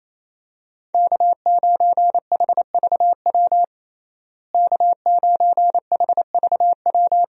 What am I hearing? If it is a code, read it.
Morse code, heard as a keyed tone, about 27 wpm, 700 Hz. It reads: K9HVW K9HVW
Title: USA Call Signs - 5 Characters - Repeated 2x - 28wpm